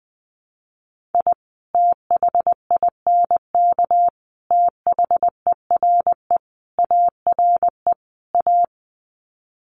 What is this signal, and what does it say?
Morse code, heard as a keyed tone, about 20 wpm, 700 Hz. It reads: I THINK THERE ARE A
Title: Sets of 5 Words Encore2 20wpm